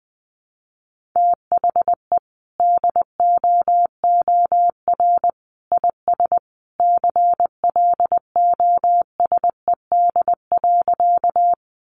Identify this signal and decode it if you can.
Morse code, heard as a keyed tone, about 20 wpm, 700 Hz. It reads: THE DOOR IS CLOSED.
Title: Sentences from Top 1400 Words 20wpm